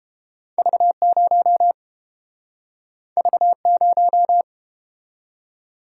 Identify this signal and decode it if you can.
Morse code, heard as a keyed tone, about 31 wpm, 700 Hz. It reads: V0 V0